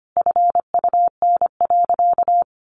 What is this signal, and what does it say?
Morse code, heard as a keyed tone, about 25 wpm, 700 Hz. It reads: FUN.
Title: Easy Phrases 25wpm